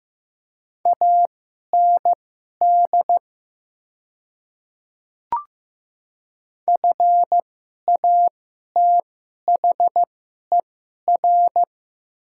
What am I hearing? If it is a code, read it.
Morse code, heard as a keyed tone, about 15 wpm, spaced as for 11 wpm, 700 Hz. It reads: AND FATHER
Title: 2x Character Spacing - Top 500 Words - 15wpm